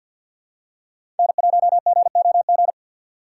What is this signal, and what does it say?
Morse code, heard as a keyed tone, about 52 wpm, 700 Hz. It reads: N1GOG
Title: USA Call Signs - 3x Character Spacing - 50wpm